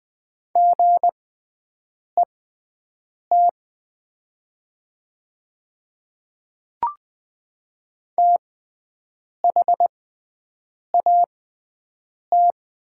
Morse code, heard as a keyed tone, about 20 wpm, spaced as for 7 wpm, 700 Hz. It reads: GET THAT
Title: 6x Character Spacing - Top 300 Words - 20wpm